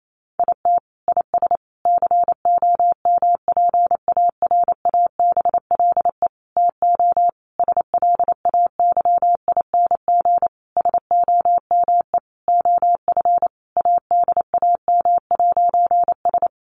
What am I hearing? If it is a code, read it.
Morse code, heard as a keyed tone, about 28 wpm, 700 Hz. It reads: IT IS COMPARABLE TO SLAYING SOME OF ADAM'S
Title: A Sound Of Thunder 28wpm